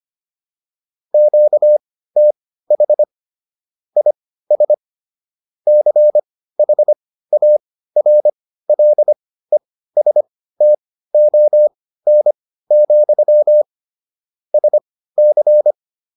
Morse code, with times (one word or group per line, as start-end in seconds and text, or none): 1.1-3.0 QTH
4.0-4.7 IS
5.7-13.6 CHARLESTON,
14.5-15.7 SC